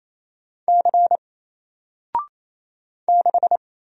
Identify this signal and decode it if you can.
Morse code, heard as a keyed tone, about 27 wpm, 700 Hz. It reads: C 6
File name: Single-Letter-Number-28wpm